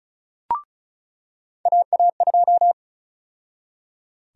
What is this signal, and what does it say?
Morse code, heard as a keyed tone, about 34 wpm, 700 Hz. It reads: AA2
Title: USA Call Sign Prefixes - 35wpm